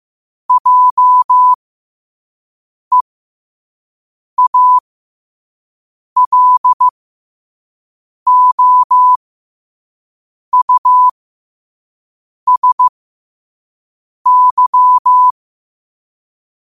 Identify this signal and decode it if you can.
Morse code, heard as a keyed tone, about 15 wpm, spaced as for 5 wpm, 1000 Hz. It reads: JEALOUSY
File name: qotdfiles / 5wpm